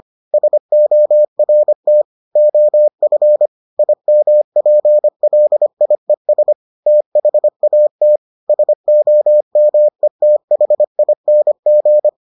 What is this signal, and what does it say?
Morse code, heard as a keyed tone, about 25 wpm, 600 Hz. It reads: SORT OF IMPLIES THAT SOMETHING